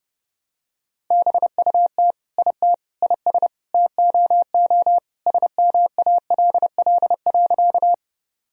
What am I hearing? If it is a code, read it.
Morse code, heard as a keyed tone, about 32 wpm, 700 Hz. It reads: BUT IT IS TOO SMALL.
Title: Sentences from Top 1000 Words - Review Part 1 - 30wpm